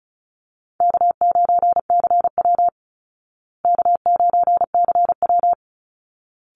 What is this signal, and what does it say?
Morse code, heard as a keyed tone, about 34 wpm, 700 Hz. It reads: K9CW K9CW
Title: Easy Call Signs Repeated 2x - 35wpm